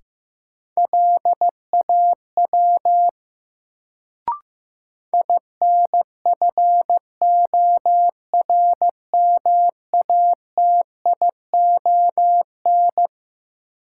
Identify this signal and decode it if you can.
Morse code, heard as a keyed tone, about 15 wpm, 700 Hz. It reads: LAW INFORMATION
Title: Sentences from Top 600 Words - New Word Drill - 15wpm